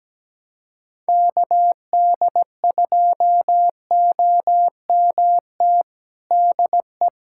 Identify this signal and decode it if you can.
Morse code, heard as a keyed tone, about 17 wpm, 700 Hz. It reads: KD2OMT DE
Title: USA Callsign DE Callsign 17wpm